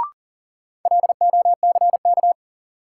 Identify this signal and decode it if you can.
Morse code, heard as a keyed tone, about 40 wpm, 700 Hz. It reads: LOCK
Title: Sentences from Top 1300 Words - New Word Drill - 40wpm